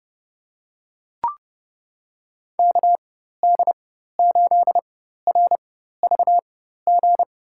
Morse code, heard as a keyed tone, about 30 wpm, spaced as for 14 wpm, 700 Hz. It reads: KD8RVG
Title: USA Call Signs - 6 Characters - 4x Character Spacing - 30wpm